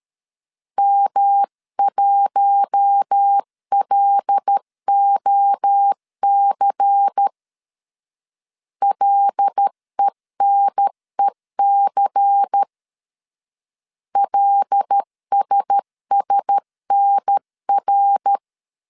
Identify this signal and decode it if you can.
Morse code, heard as a keyed tone, about 13 wpm, 800 Hz. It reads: M1LOC LENEC LSSNR
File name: CW / DET / 1 6 1500C S C 12wpm 3533